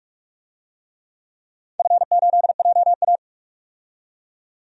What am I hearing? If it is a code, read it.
Morse code, heard as a keyed tone, about 42 wpm, 700 Hz. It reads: F8JA
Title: DXCC-100 Call Signs - 4 Characters - 45wpm